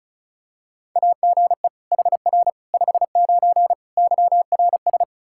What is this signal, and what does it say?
Morse code, heard as a keyed tone, about 34 wpm, 700 Hz. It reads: AGE HR 59 YRS
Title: Practice QSO - Exchange 3 - More Info - 35wpm